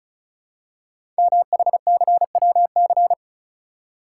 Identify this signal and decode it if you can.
Morse code, heard as a keyed tone, about 36 wpm, 700 Hz. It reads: MHCWC
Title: Code Groups - 5 Letter:Numbers - 35wpm